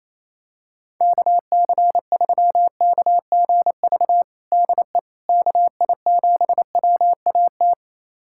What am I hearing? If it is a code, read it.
Morse code, heard as a keyed tone, about 28 wpm, 700 Hz. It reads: KC3KGV DE KI7WAT
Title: USA Callsign DE Callsign 28wpm